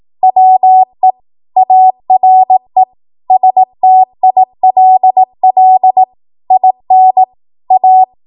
Morse code, heard as a keyed tone, about 18 wpm, 750 Hz. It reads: WE ARE STILL IN A
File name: Morse / 18WPM